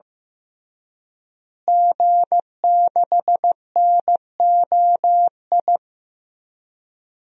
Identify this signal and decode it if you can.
Morse code, heard as a keyed tone, about 15 wpm, 700 Hz. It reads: G6NOI
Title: DXCC-100 Call Signs - 5 Characters - Repeated 3x - 15wpm